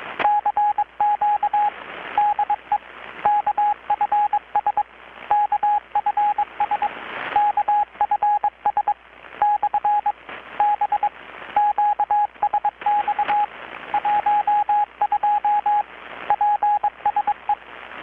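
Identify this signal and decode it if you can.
Morse code, heard as a keyed tone, about 22 wpm, 830 Hz, with radio noise. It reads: CQ DE KFS KFS KFS / B QSX 12 PSE
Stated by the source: Morse from KPH's Night of Nights XX maritime radiotelegraphy event, received in Chiapas, Mexico about 3,800 km away on a Kenwood R-600 with a grounded 10 m long-wire antenna.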